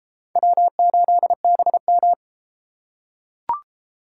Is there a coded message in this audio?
Yes: Morse code from 0.4 to 2.1 s, heard as a keyed tone, about 33 wpm, 700 Hz. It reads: W8BM